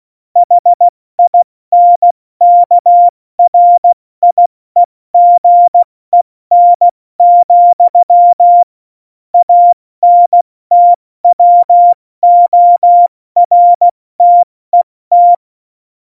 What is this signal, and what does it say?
Morse code, heard as a keyed tone, about 16 wpm, 700 Hz. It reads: HINKRIEGEN, ANTWORTET
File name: Morsekurs / Witze / Witz 15. September 2020, 70 F80